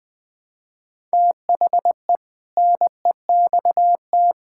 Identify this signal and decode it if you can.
Morse code, heard as a keyed tone, about 20 wpm, 700 Hz. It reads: THE NEXT
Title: Sentences with 7-Word Phrases - Spoken Beginning - 20wpm